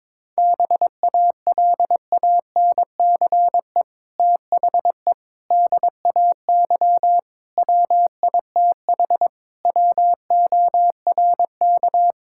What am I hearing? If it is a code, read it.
Morse code, heard as a keyed tone, about 22 wpm, 700 Hz. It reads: BALANCE THE DAY WITH WORK
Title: Sentences from Top 1000 Words - Review Part 6 - 22wpm